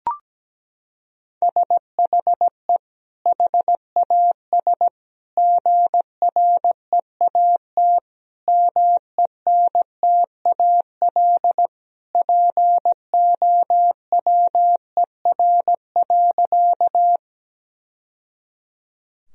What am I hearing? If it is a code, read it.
Morse code, heard as a keyed tone, about 17 wpm, 700 Hz. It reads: SHE HAS GREAT MENTAL POWER.
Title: Sentences from Top 1000 Words - Review Part 4 - 17wpm